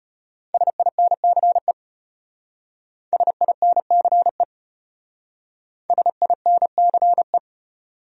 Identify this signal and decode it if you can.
Morse code, heard as a keyed tone, about 31 wpm, 700 Hz. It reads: SINCE SINCE SINCE
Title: Speed-Racing Top 300 Words 25wpm